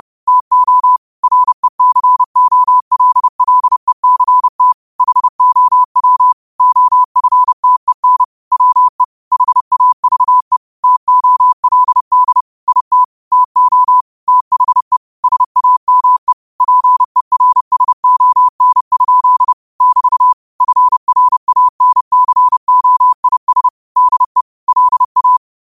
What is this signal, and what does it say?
Morse code, heard as a keyed tone, about 30 wpm, 1000 Hz. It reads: TO RECOLLECT HOW OFTEN WE HAVE TOLD IT TO THE SAME PERSON? <BT> FRANCOIS DE LA
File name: qotdfiles / 30wpm